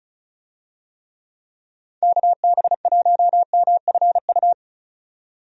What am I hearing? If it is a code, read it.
Morse code, heard as a keyed tone, about 34 wpm, 700 Hz. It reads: KB1MFU
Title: USA Call Signs - 6 Characters - 35wpm